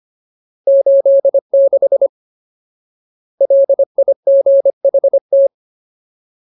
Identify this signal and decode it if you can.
Morse code, heard as a keyed tone, about 25 wpm, 550 Hz. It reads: 86 LIGHT
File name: Ultimate Word List / 25WPM Ultimate